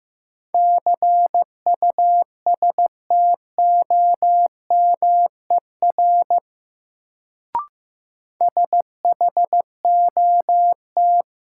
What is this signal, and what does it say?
Morse code, heard as a keyed tone, about 15 wpm, 700 Hz. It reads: CUSTOMER SHOT
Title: Sentences from Top 1100 Words - New Word Drill - 15wpm